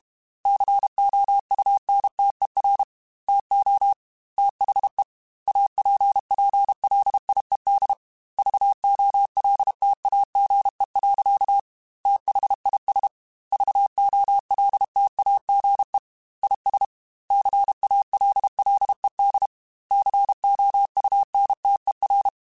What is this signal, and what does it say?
Morse code, heard as a keyed tone, about 32 wpm, 800 Hz. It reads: COUNTER TO THE APPLIED VOLTAGE. THIS VOLTAGE IS CALLED COUNTER
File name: Technical Subjects / Technical Subjects-mp3s / XL @30